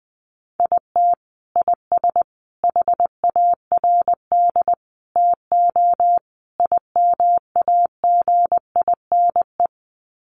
Morse code, heard as a keyed tone, about 20 wpm, 700 Hz. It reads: IT IS HARD TO IMAGINE
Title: Sets of 5 Words Encore2 20wpm